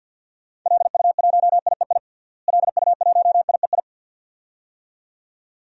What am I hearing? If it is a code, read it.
Morse code, heard as a keyed tone, about 48 wpm, 700 Hz. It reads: LU1IEI LU1IEI
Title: DXCC-100 Call Signs - 6 Characters - Repeated 2x - 50wpm